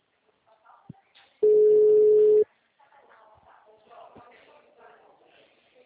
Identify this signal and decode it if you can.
Ringback tone (Europe, 425 Hz).
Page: Atibaia - SP